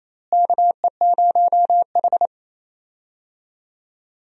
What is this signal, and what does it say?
Morse code, heard as a keyed tone, about 28 wpm, 700 Hz. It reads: KE0H
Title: USA Call Signs - 4 Characters - Repeated 2x - 28wpm